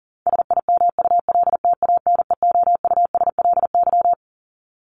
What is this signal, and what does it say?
Morse code, heard as a keyed tone, about 40 wpm, 700 Hz. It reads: SIMULTANEOUSLY
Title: LY Suffix - Top 100 Words with Base Words 40wpm